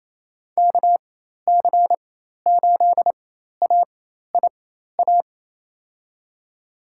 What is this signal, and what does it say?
Morse code, heard as a keyed tone, about 28 wpm, spaced as for 13 wpm, 700 Hz. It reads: KC8AIA
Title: USA Call Signs - 4x Character Spacing - 28wpm